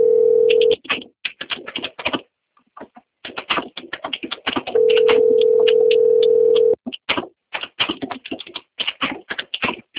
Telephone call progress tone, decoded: ringback tone

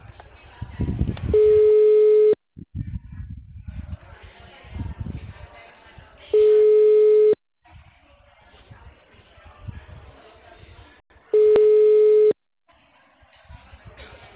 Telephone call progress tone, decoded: ringback tone

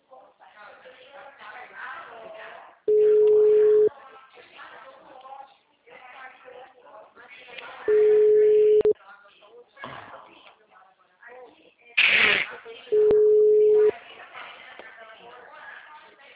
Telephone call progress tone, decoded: ringback tone

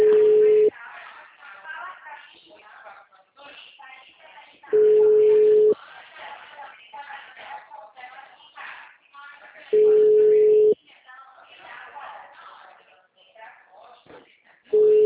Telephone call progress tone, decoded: ringback tone

